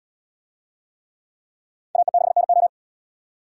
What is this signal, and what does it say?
Morse code, heard as a keyed tone, about 69 wpm, 700 Hz. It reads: AE6NO